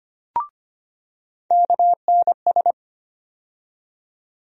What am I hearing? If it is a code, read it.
Morse code, heard as a keyed tone, about 25 wpm, 700 Hz. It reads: KNS
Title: USA Call Sign Postfixes - 3 Letters 25wpm